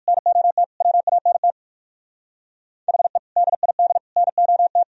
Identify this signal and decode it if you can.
Morse code, heard as a keyed tone, about 63 wpm, spaced as for 55 wpm, 700 Hz. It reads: NOT WANT HE DID NOT
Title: Speed-Racing - Sets of 4 Words 45wpm